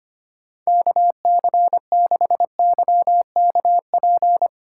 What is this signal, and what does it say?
Morse code, heard as a keyed tone, about 25 wpm, 700 Hz. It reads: KC6YKP
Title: USA Call Signs - 6 Characters - 2x Character Spacing - 25wpm